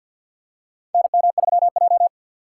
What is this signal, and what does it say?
Morse code, heard as a keyed tone, about 47 wpm, 700 Hz. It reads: NM3J
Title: USA Call Signs - 4 Characters - 3x Character Spacing - 50wpm